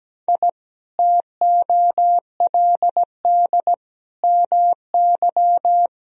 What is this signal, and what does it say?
Morse code, heard as a keyed tone, about 17 wpm, 700 Hz. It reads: I TOLD MY
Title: Sentences from Top 1000 Words - Review Part 1 - 17wpm